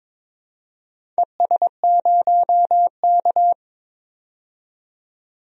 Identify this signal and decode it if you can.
Morse code, heard as a keyed tone, about 22 wpm, 700 Hz. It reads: ES0K
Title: DXCC-100 Call Signs - 4 Characters - 22wpm